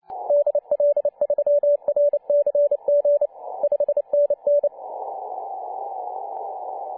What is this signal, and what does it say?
Morse code, heard as a keyed tone, about 30 wpm, 580 Hz. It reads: DL3RCG 5NN
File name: 2013/Sep/05 2136z T8GM 14.023MHz CW